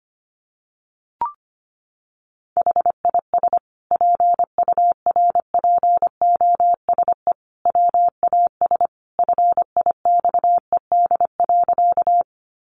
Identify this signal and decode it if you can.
Morse code, heard as a keyed tone, about 25 wpm, 700 Hz. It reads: HIS PURPOSE WAS FIXED.
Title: Sentences from Top 1100 Words 25wpm